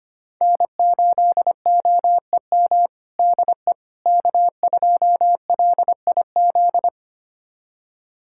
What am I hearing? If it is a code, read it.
Morse code, heard as a keyed tone, about 25 wpm, 700 Hz. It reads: N8OEM DE K2LIZ
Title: USA Callsign DE Callsign 25wpm